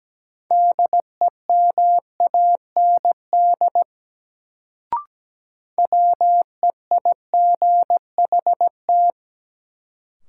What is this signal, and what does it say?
Morse code, heard as a keyed tone, about 17 wpm, 700 Hz. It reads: DEMAND WEIGHT
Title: Sentences from Top 800 Words - New Word Drill - 17wpm